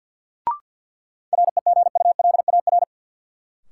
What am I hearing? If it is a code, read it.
Morse code, heard as a keyed tone, about 53 wpm, 700 Hz. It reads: REGULAR
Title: Sentences from Top 900 Words - New Word Drill - 50wpm